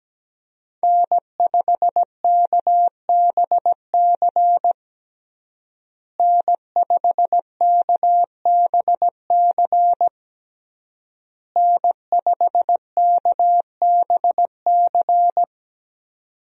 Morse code, heard as a keyed tone, about 17 wpm, 700 Hz. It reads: N5KBC N5KBC N5KBC
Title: USA Call Signs - 5 Characters - Repeated 3x - 17wpm